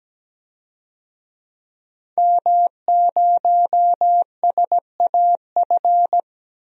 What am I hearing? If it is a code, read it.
Morse code, heard as a keyed tone, about 17 wpm, 700 Hz. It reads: M0SAF
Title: DXCC-100 Call Signs - 5 Characters - Repeated 2x - 17wpm